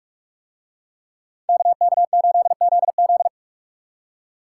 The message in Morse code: KK8ZZ

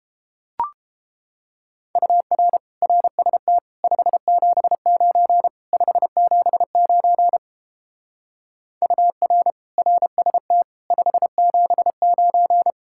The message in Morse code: UR RST 579 579 UR RST 579

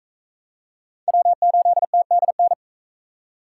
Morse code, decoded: W8TDN

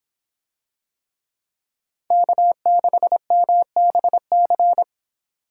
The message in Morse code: K6MBC

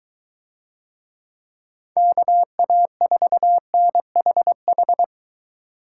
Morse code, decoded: KA4NHH

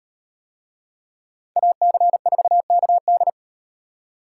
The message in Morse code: AC4KD

